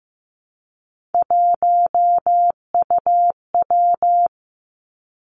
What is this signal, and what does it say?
Morse code, heard as a keyed tone, about 15 wpm, 700 Hz. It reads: 1UW
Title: Three Challenging Letter-Number Combinations 15wpm